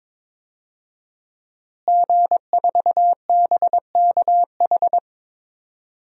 Morse code, heard as a keyed tone, about 22 wpm, 700 Hz. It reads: G4BKH
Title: DXCC-100 Call Signs - 5 Characters - Repeated 3x - 22wpm